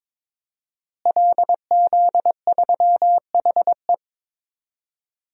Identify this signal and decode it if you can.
Morse code, heard as a keyed tone, about 22 wpm, 700 Hz. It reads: LZ3HE
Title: DXCC-100 Call Signs - 5 Characters - Repeated 2x - 22wpm